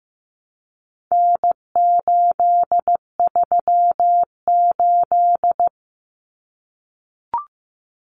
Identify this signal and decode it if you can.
Morse code, heard as a keyed tone, about 15 wpm, 700 Hz. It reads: N838
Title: Common Cut Numbers - 4-Digit 15wpm